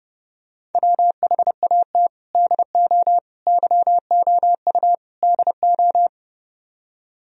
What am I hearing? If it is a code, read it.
Morse code, heard as a keyed tone, about 30 wpm, 700 Hz. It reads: WHAT DO YOU DO